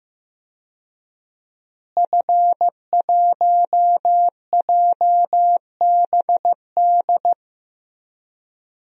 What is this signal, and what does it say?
Morse code, heard as a keyed tone, about 15 wpm, 700 Hz. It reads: F1JBD
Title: DXCC-100 Call Signs - 5 Characters - Repeated 3x - 15wpm